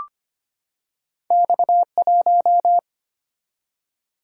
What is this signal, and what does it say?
Morse code, heard as a keyed tone, about 25 wpm, 700 Hz. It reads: X1
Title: Single-Letter Single-Number 25wpm